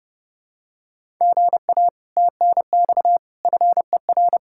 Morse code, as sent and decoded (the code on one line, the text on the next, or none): --. .- / - -. -..- / ..-. .-.
GA TNX FER